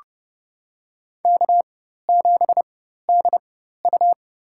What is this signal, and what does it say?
Morse code, heard as a keyed tone, about 31 wpm, spaced as for 14 wpm, 700 Hz. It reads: K7DU
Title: USA Call Signs - 4 Characters - 4x Character Spacing - 30wpm